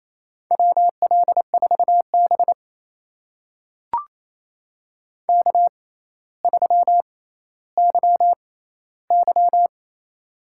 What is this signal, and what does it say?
Morse code, heard as a keyed tone, about 28 wpm, spaced as for 14 wpm, 700 Hz. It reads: WL4B K3YY
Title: USA Call Signs - 4 Characters - 6x Character Spacing - 28wpm